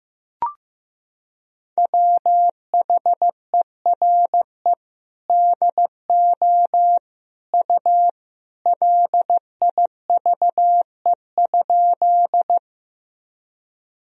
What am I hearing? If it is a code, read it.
Morse code, heard as a keyed tone, about 15 wpm, 700 Hz. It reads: WHERE DO U LIVE?
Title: Sentences from Top 200 Words and Abbreviations 15wpm